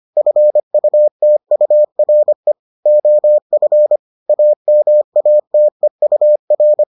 Morse code, heard as a keyed tone, about 25 wpm, 600 Hz. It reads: FUTURE OF AMATEUR